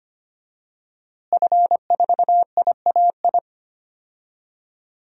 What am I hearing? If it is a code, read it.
Morse code, heard as a keyed tone, about 25 wpm, 700 Hz. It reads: F4IAI
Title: DXCC-100 Call Signs - 5 Characters - Repeated 3x - 25wpm